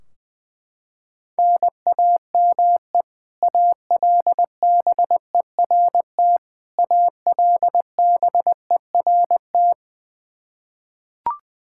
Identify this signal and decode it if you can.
Morse code, heard as a keyed tone, about 20 wpm, 700 Hz. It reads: NAME ALBERT ALBERT
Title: Practice QSO - Exchange 1 - Beginning - 20wpm